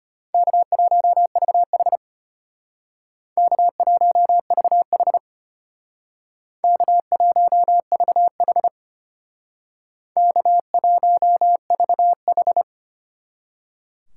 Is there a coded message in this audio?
Yes: Morse code from 0.3 to 12.6 s, heard as a keyed tone, about 32 wpm, 700 Hz. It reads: K1VH K1VH K1VH K1VH